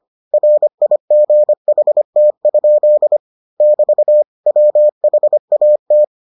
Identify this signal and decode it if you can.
Morse code, heard as a keyed tone, about 25 wpm, 600 Hz. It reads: RIGHT? <BT> WHAT